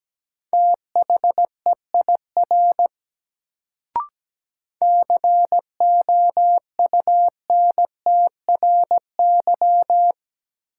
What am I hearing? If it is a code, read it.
Morse code, heard as a keyed tone, about 17 wpm, 700 Hz. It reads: THEIR COUNTRY
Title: Sets of 2 Words - New Word Drill - 17wpm